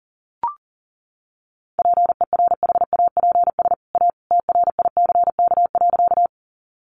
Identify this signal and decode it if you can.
Morse code, heard as a keyed tone, about 40 wpm, 700 Hz. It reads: PERHAPS A TRICK.
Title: Sentences from Top 1500 Words 40wpm